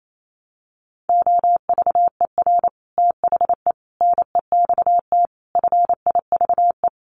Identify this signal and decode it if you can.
Morse code, heard as a keyed tone, about 28 wpm, 700 Hz. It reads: OVER THE NEXT FIVE
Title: Sets of 5 Words 28wpm